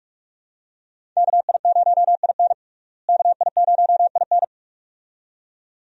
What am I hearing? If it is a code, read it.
Morse code, heard as a keyed tone, about 42 wpm, 700 Hz. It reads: KI0IN KI0IN
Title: USA Call Signs - 5 Characters - Repeated 2x - 45wpm